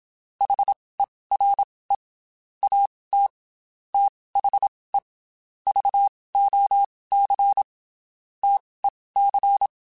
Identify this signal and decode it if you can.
Morse code, heard as a keyed tone, about 27 wpm, spaced as for 19 wpm, 800 Hz. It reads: HERE AT THE VOC TEC
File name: QSOs / QSOs-mp3s / WD0FBT 1@25